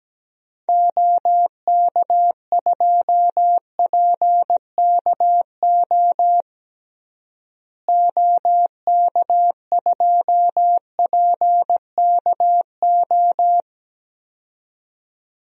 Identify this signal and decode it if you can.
Morse code, heard as a keyed tone, about 17 wpm, 700 Hz. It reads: OK2PKO OK2PKO
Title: DXCC-100 Call Signs - 6 Characters - Repeated 2x - 17wpm